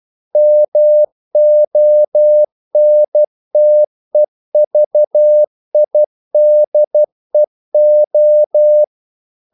Morse code, heard as a keyed tone, about 12 wpm, 600 Hz. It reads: MONTEVIDEO
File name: CW / Practice / NationalCapitals12